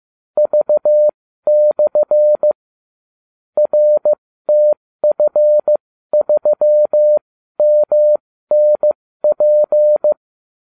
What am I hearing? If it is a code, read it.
Morse code, heard as a keyed tone, about 15 wpm, spaced as for 13 wpm, 600 Hz. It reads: V/ RTF3MNP